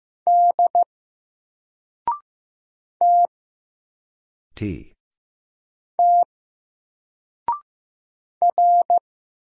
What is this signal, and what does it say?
Morse code, heard as a keyed tone, about 15 wpm, 700 Hz. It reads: D T T R
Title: Single Letter-Number 15wpm